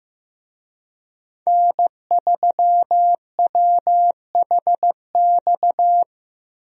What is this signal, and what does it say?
Morse code, heard as a keyed tone, about 15 wpm, 700 Hz. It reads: N3WHX
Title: USA Call Signs - 5 Characters - Repeated 3x - 15wpm